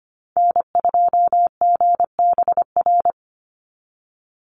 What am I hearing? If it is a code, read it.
Morse code, heard as a keyed tone, about 25 wpm, 700 Hz. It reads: N2GBR
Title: NA SOTA Activator Callsigns 25wpm